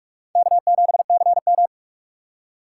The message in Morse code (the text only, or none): K7CM